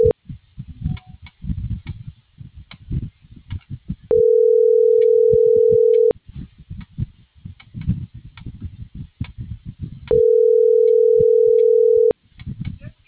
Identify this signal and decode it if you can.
Ringback tone (North America), 440 + 480 Hz.